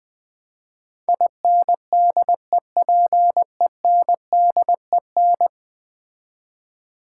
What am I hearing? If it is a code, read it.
Morse code, heard as a keyed tone, about 20 wpm, 700 Hz. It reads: INDEPENDEN
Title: Build A Word - 5 Letters At A Time - 12-Letter Words - 20wpm